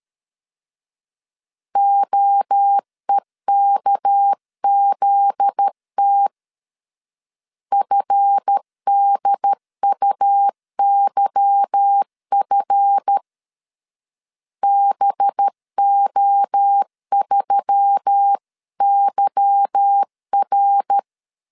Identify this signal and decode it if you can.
Morse code, heard as a keyed tone, about 13 wpm, 800 Hz. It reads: OEKZT FDUYF BO3YR
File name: LEZIONI / 8 YJ45 / 8 Y 12wpm 2228